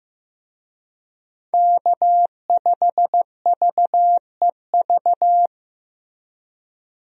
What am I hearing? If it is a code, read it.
Morse code, heard as a keyed tone, about 15 wpm, 700 Hz. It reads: K5VEV
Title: USA Call Signs - 5 Characters - Repeated 3x - 15wpm